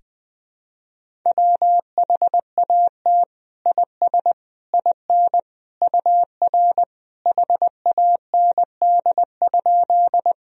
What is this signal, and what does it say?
Morse code, heard as a keyed tone, about 20 wpm, 700 Hz. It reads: WHAT IS IN UR HAND?
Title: Sentences from Top 200 Words and Abbreviations 20wpm